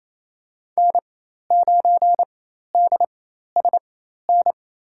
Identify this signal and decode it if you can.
Morse code, heard as a keyed tone, about 27 wpm, spaced as for 13 wpm, 700 Hz. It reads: N9DSN
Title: USA Call Signs - 5 Characters - 4x Character Spacing - 28wpm